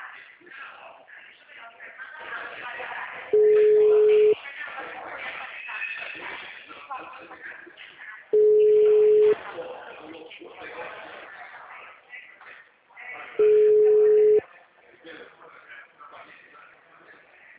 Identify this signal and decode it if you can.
Ringback tone (Europe, 425 Hz), with speech.